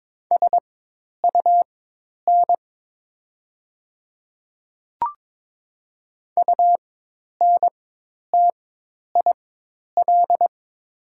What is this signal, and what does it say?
Morse code, heard as a keyed tone, about 23 wpm, spaced as for 10 wpm, 700 Hz. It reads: SUN UNTIL
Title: 4x Character Spacing - Top 300 Words - 22wpm